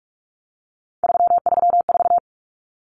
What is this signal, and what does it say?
Morse code, heard as a keyed tone, about 44 wpm, 700 Hz. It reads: 334
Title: Common Cut Numbers - 3-Digit 45wpm